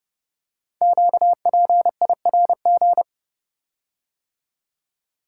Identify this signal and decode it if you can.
Morse code, heard as a keyed tone, about 30 wpm, 700 Hz. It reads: QPIRG
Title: Code Groups - 5 Letters - 30wpm